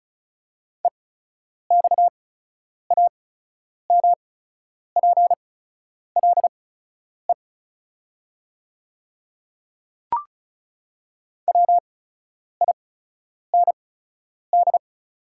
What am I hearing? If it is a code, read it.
Morse code, heard as a keyed tone, about 36 wpm, spaced as for 10 wpm, 700 Hz. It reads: EXAMPLE WIND
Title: 8x Character Spacing - Top 500 Words - 35wpm